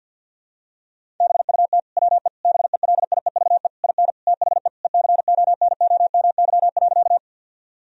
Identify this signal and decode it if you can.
Morse code, heard as a keyed tone, about 50 wpm, 700 Hz. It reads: BUT WE BELIEVE IN THE ECONOMY.
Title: Sentences from Top 1000 Words - Review Part 3 - 50wpm